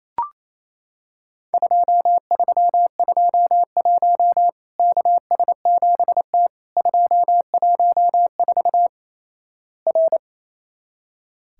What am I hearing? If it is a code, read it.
Morse code, heard as a keyed tone, about 28 wpm, 700 Hz. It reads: 2321 KS7T 214 R
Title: Worked All Europe DX Contest - QTC Exchange - 28wpm